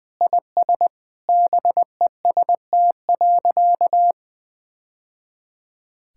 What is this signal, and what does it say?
Morse code, heard as a keyed tone, about 20 wpm, 700 Hz. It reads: IS BEST.
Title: Sentences with 2-Word Phrases - Spoken Beginning - 20wpm